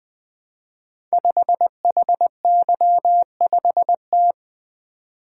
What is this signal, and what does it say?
Morse code, heard as a keyed tone, about 20 wpm, 700 Hz. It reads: 5HY5T
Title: Code Groups - 5 Letter:Numbers - 20wpm